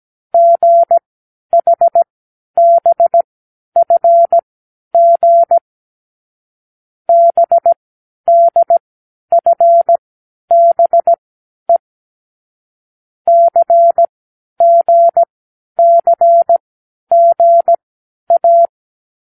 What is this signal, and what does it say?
Morse code, heard as a keyed tone, about 17 wpm, spaced as for 10 wpm, 680 Hz. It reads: GHBFG BDFBE CGCGA
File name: k7qo / Lesson12(A-H)